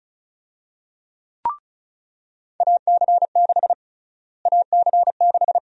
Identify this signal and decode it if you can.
Morse code, heard as a keyed tone, about 36 wpm, 700 Hz. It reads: AC6 AC6
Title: USA Call Sign Prefixes - Repeated 2x - 35wpm